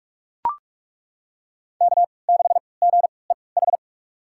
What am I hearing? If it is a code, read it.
Morse code, heard as a keyed tone, about 41 wpm, spaced as for 25 wpm, 700 Hz. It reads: K6GEH
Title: USA Call Signs - 5 Characters - 3x Character Spacing - 45wpm